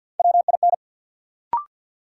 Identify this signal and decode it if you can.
Morse code, heard as a keyed tone, about 53 wpm, 700 Hz. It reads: WIN
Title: Sentences from Top 1500 Words - New Word Drill Alt - 50wpm